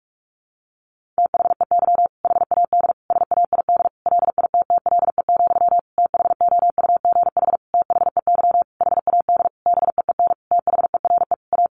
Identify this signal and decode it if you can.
Morse code, heard as a keyed tone, about 44 wpm, 700 Hz. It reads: THEY HAD SAID LITTLE, THOUGH THEY HAD BEEN THERE A
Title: Sentences from Top 400 Words 45wpm